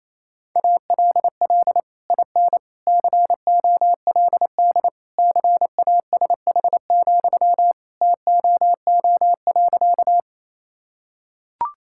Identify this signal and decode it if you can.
Morse code, heard as a keyed tone, about 28 wpm, 700 Hz. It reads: ALL IN COLD CASH, TOO.
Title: Sentences from Top 1000 Words - Review Part 5 - 28wpm